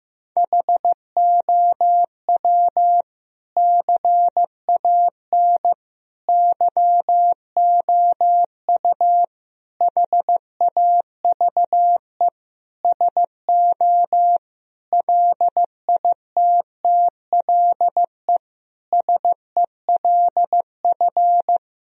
Morse code, heard as a keyed tone, about 15 wpm, 700 Hz. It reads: HOW CAN YOU HAVE SO LITTLE SELF
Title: Sentences from Top 1000 Words - Review Part 1 - 15wpm